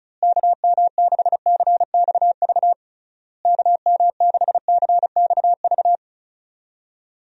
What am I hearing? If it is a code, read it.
Morse code, heard as a keyed tone, about 36 wpm, 700 Hz. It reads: KM6CXV KM6CXV
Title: USA Call Signs - 6 Characters - Repeated 3x - 35wpm